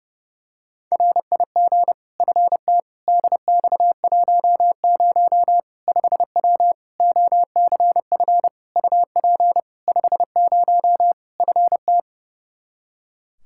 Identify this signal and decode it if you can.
Morse code, heard as a keyed tone, about 32 wpm, 700 Hz. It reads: RIG FT DX10 5W OCF UP 50 FT
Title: Practice QSO - Exchange 2 - Info - 30wpm